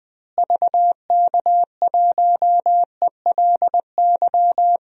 Morse code, heard as a keyed tone, about 20 wpm, 700 Hz. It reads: VK1ELY
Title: DXCC-100 Call Signs - 6 Characters - Repeated 3x - 20wpm